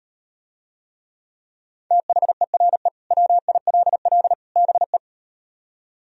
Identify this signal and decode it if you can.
Morse code, heard as a keyed tone, about 38 wpm, 700 Hz. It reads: THERE WILL BE